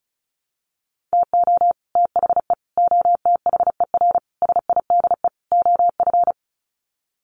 Morse code, heard as a keyed tone, about 34 wpm, 700 Hz. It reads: TO THE OTHER SIDE OF
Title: Sets of 5 Words Encore 35wpm